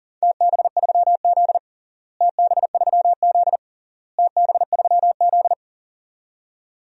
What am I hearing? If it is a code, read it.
Morse code, heard as a keyed tone, about 40 wpm, 700 Hz. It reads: TB3Z TB3Z TB3Z
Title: DXCC-100 Call Signs - 4 Characters - Repeated 3x - 40wpm